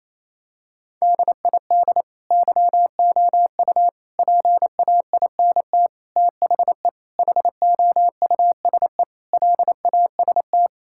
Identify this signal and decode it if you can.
Morse code, heard as a keyed tone, about 28 wpm, 700 Hz. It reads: DID YOU PAINT THE HOUSE LAST
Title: Sentences from Top 1000 Words - Review Part 6 - 28wpm